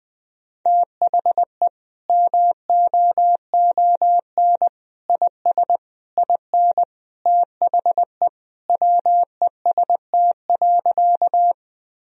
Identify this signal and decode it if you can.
Morse code, heard as a keyed tone, about 20 wpm, 700 Hz. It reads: THE MOON IS IN THE WEST.
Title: Sentences from Top 1000 Words - Review Part 3 - 20wpm